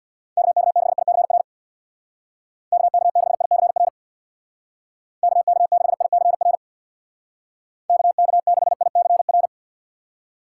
Morse code, heard as a keyed tone, about 65 wpm, 700 Hz. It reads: KK6ICR KK6ICR KK6ICR KK6ICR